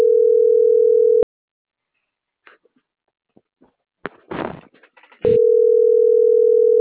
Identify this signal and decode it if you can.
Ringback tone (North America), 440 + 480 Hz.